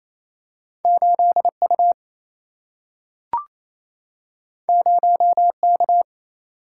Morse code, heard as a keyed tone, about 28 wpm, 700 Hz. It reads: 8U 0K
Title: Single-Letter Single-Number 28wpm